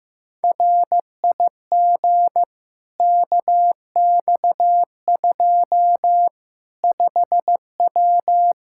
Morse code, heard as a keyed tone, about 15 wpm, 700 Hz. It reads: RIG KX2 5W
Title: Practice QSO - Exchange 2 - Info - 15wpm